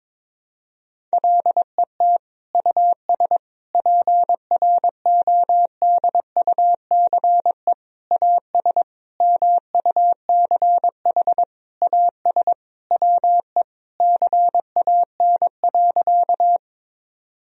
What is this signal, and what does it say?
Morse code, heard as a keyed tone, about 22 wpm, 700 Hz. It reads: LET US PRODUCE AS MUCH AS WE CAN.
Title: Sentences from Top 1000 Words - Review Part 3 - 22wpm